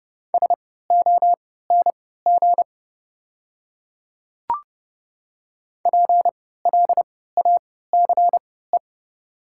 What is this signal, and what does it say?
Morse code, heard as a keyed tone, about 30 wpm, spaced as for 17 wpm, 700 Hz. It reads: SONG PLACE
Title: 3x Character Spacing - Top 500 Words - 30wpm